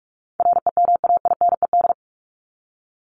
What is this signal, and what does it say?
Morse code, heard as a keyed tone, about 44 wpm, 700 Hz. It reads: REMAINED
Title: ED Suffix - Top 100 Words with Base Words 45wpm